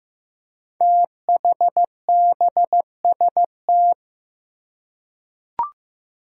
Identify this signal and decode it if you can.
Morse code, heard as a keyed tone, about 15 wpm, 700 Hz. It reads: THBST
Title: Code Groups - 5 Letters - 15wpm